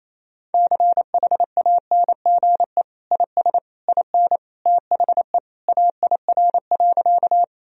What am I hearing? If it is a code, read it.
Morse code, heard as a keyed tone, about 28 wpm, 700 Hz. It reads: CHANGE IS IN THE AIR.
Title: Sentences from Top 1000 Words - Review Part 1 - 28wpm